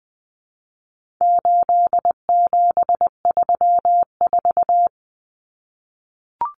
Morse code, heard as a keyed tone, about 20 wpm, 700 Hz. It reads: 8734
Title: Common Cut Numbers - 4-Digit 20wpm